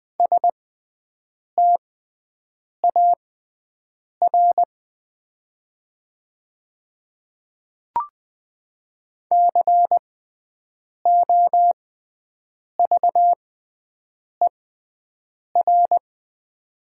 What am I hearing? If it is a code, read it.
Morse code, heard as a keyed tone, about 20 wpm, spaced as for 7 wpm, 700 Hz. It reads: STAR COVER
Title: 6x Character Spacing - Top 500 Words - 20wpm